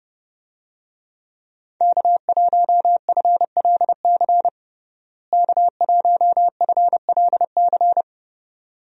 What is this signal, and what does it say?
Morse code, heard as a keyed tone, about 30 wpm, 700 Hz. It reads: K1FLC K1FLC
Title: USA Call Signs - 5 Characters - Repeated 2x - 30wpm